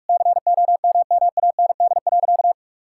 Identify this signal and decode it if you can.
Morse code, heard as a keyed tone, about 48 wpm, 700 Hz. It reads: COMMAND.
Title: Sentences from Top 1600 Words 45wpm